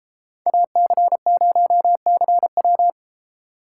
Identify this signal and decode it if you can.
Morse code, heard as a keyed tone, about 31 wpm, 700 Hz. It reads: AC0CW